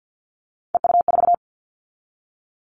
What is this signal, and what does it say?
Morse code, heard as a keyed tone, about 50 wpm, 700 Hz. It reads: EU4
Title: Callsign Prefixes for Top 100 DXCC - 50wpm